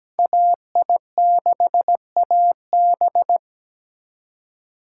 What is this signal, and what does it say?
Morse code, heard as a keyed tone, about 20 wpm, 700 Hz. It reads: AI6AB
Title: Speed Racing - USA 2x2 Call Signs 17wpm